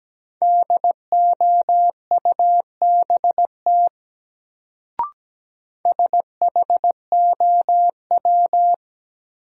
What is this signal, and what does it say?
Morse code, heard as a keyed tone, about 17 wpm, 700 Hz. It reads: DOUBT SHOW
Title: Sets of 5 Words - New Word Drill - 17wpm